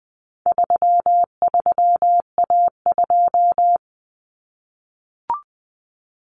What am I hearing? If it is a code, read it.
Morse code, heard as a keyed tone, about 20 wpm, 700 Hz. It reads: 33A2
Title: Common Cut Numbers - 4-Digit 20wpm